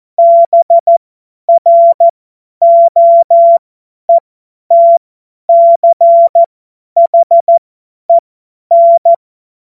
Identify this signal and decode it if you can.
Morse code, heard as a keyed tone, about 14 wpm, spaced as for 10 wpm, 680 Hz. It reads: BROETCHEN
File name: Morsekurs / Witze / Witz 07.November 2023 50 F70